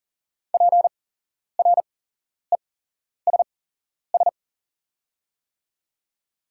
Morse code, heard as a keyed tone, about 40 wpm, spaced as for 11 wpm, 690 Hz. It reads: PRESS
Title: 8x Character Spacing - Top 500 Words - 40wpm